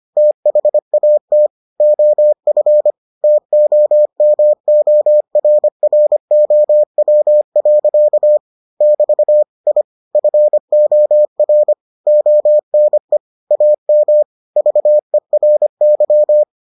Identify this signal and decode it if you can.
Morse code, heard as a keyed tone, about 25 wpm, 600 Hz. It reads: THAT OF TOMORROW. <BT> I FOR ONE AM VERY